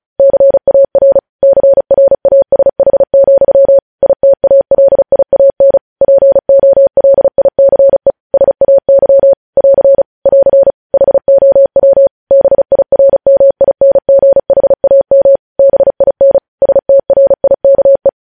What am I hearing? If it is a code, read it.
Morse code, heard as a keyed tone, about 36 wpm, 550 Hz. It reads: CAR CRASH, ITALIAN POLICE SAY <AR> <AR> HOW BIRMINGHAM BIN STRIKE
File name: podcast / News headlines-1x-35x35